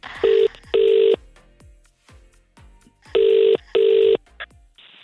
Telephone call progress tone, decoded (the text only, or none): ringback tone